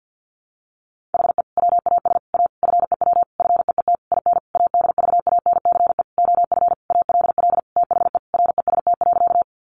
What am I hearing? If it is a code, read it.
Morse code, heard as a keyed tone, about 50 wpm, 700 Hz. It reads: HE WAS A FEW FEET IN ADVANCE OF ALL THE REST.
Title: Sentences from Top 1400 Words 50wpm